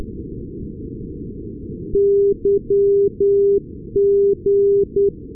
Morse code, heard as a keyed tone, about 10 wpm, 400 Hz. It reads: YG
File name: europe / nor / nor-258 yg-396 la6lu-2020-12-kiwi regional-rb